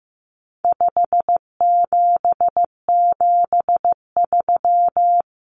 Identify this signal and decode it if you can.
Morse code, heard as a keyed tone, about 14 wpm, 700 Hz. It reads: 5773
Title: Common Cut Numbers - 4-Digit 15wpm